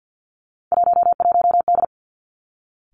Morse code, heard as a keyed tone, about 50 wpm, 700 Hz. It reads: 11D
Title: Three Challenging Letter-Number Combinations 50wpm